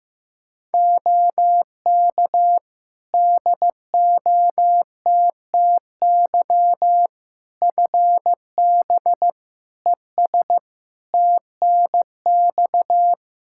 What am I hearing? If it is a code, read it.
Morse code, heard as a keyed tone, about 15 wpm, 700 Hz. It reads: OK DOTTY FB ES TNX
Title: K1USN SST QSO Practice 15wpm